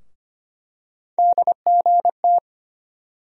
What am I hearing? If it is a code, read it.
Morse code, heard as a keyed tone, about 25 wpm, 700 Hz. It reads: DGT
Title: USA Call Sign Postfixes - 3 Letters 25wpm